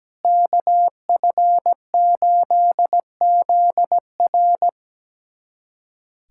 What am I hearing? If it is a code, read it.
Morse code, heard as a keyed tone, about 17 wpm, 700 Hz. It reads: KF8ZR
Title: USA Callsign DE Callsign - 5 Characters - 17wpm